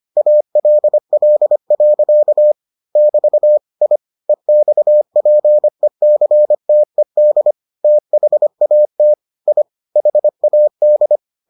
Morse code, heard as a keyed tone, about 25 wpm, 600 Hz. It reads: ALL. <BT> I EXPECTED THAT I HAD